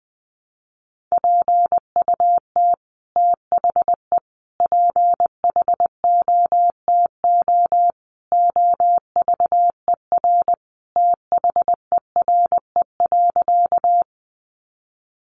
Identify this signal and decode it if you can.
Morse code, heard as a keyed tone, about 20 wpm, 700 Hz. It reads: PUT THE PHOTO OVER THERE.
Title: Sentences from Top 1200 Words 20wpm